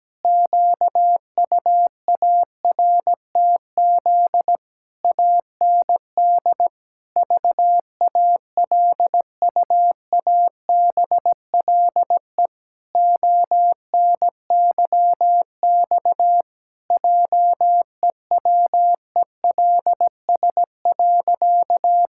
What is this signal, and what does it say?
Morse code, heard as a keyed tone, about 17 wpm, 700 Hz. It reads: QUARTZ AND VALUABLE ONYX JEWELS.
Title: Pangram Challenge v2 - 17wpm